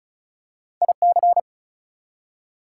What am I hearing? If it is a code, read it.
Morse code, heard as a keyed tone, about 36 wpm, 700 Hz. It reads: IC
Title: Prefixes and Suffixes 35wpm